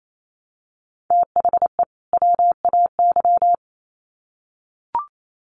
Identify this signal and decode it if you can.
Morse code, heard as a keyed tone, about 28 wpm, 700 Hz. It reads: THE WAY
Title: Sets of 2 Words 28wpm